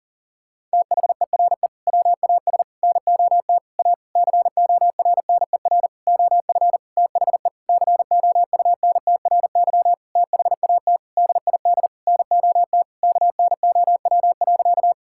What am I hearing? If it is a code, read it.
Morse code, heard as a keyed tone, about 37 wpm, 700 Hz. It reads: THERE WAS NOT A CORNER OF THE COUNTRY THAT DID NOT KNOW.
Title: Sentences from Top 1600 Words 40wpm